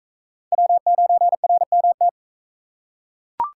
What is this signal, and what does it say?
Morse code, heard as a keyed tone, about 39 wpm, 700 Hz. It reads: W9RMT